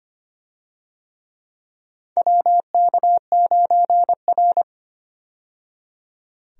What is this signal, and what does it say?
Morse code, heard as a keyed tone, about 25 wpm, 700 Hz. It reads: WK9R
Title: USA Call Signs - 4 Characters - 25wpm